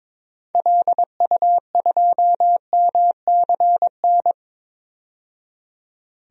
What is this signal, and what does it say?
Morse code, heard as a keyed tone, about 22 wpm, 700 Hz. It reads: LU2MCN
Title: DXCC-100 Call Signs - 6 Characters - 22wpm